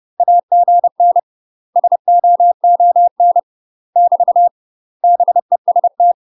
Morse code, heard as a keyed tone, about 30 wpm, 700 Hz. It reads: AGN SOON <BT> BEST
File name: UK MASTER QSOs TRAINING / 30WPM / UK MASTER QSOs 0007